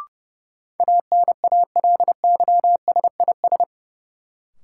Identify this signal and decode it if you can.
Morse code, heard as a keyed tone, about 29 wpm, 700 Hz. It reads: ANALYSIS
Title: Sentences from Top 1000 Words - Review Part 5 - New Word Drill - 30wpm